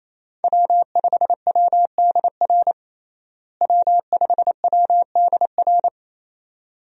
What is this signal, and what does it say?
Morse code, heard as a keyed tone, about 28 wpm, 700 Hz. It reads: W5WDR W5WDR
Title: USA Call Signs - 5 Characters - Repeated 2x - 28wpm